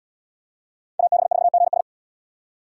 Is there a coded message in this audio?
Yes: Morse code from 1.0 to 1.8 s, heard as a keyed tone, about 69 wpm, 700 Hz. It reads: AB4GS